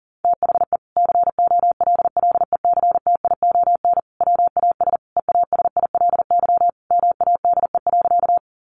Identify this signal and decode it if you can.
Morse code, heard as a keyed tone, about 40 wpm, 700 Hz. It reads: THE COLLECTION WAS EASILY MADE.
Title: Sentences from Top 1200 Words 40wpm